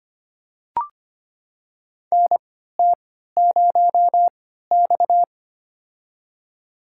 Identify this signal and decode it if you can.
Morse code, heard as a keyed tone, about 25 wpm, spaced as for 14 wpm, 700 Hz. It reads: NT0X
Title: USA Call Signs - 4 Characters - 3x Character Spacing - 25wpm